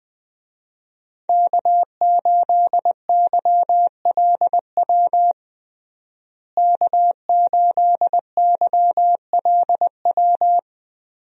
Morse code, heard as a keyed tone, about 20 wpm, 700 Hz. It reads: K8YLW K8YLW
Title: USA Call Signs - 5 Characters - Repeated 3x - 20wpm